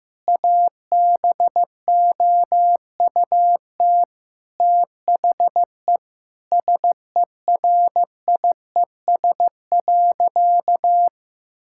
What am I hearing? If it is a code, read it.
Morse code, heard as a keyed tone, about 16 wpm, 700 Hz. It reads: ABOUT THE SERIES.
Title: Sentences from Top 1000 Words - Review Part 5 - 15wpm